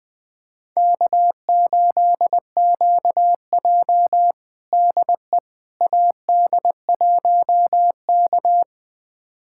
Morse code, heard as a keyed tone, about 21 wpm, 700 Hz. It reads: K8QJ DE AD1K
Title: USA Callsign DE Callsign - 4 Characters - 20wpm